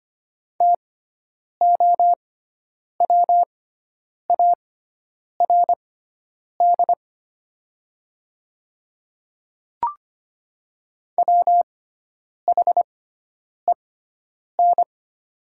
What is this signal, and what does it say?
Morse code, heard as a keyed tone, about 25 wpm, spaced as for 9 wpm, 700 Hz. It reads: TOWARD WHEN
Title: 6x Character Spacing - Top 500 Words - 25wpm